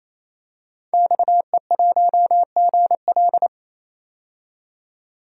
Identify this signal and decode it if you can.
Morse code, heard as a keyed tone, about 28 wpm, 700 Hz. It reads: XE1GL
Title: DXCC-100 Call Signs - 5 Characters - 28wpm